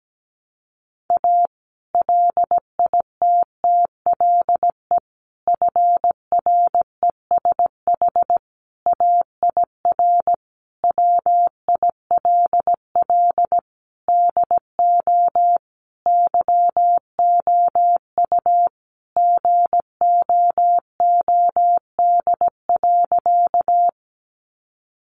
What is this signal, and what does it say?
Morse code, heard as a keyed tone, about 17 wpm, 700 Hz. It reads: A LITTLE FRESH AIR WILL DO YOU GOOD.
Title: Sentences from Top 1200 Words 17wpm